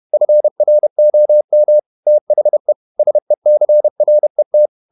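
Morse code, heard as a keyed tone, about 31 wpm, 600 Hz. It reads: FROM THE SECRET
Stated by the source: Morse code at 30 wpm, the novel Never by Ken Follett in English, from CW med Gnister.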